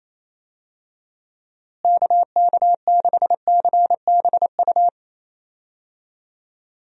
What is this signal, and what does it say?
Morse code, heard as a keyed tone, about 28 wpm, 700 Hz. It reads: KK6CBU
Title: USA Call Signs - 6 Characters - Repeated 3x - 28wpm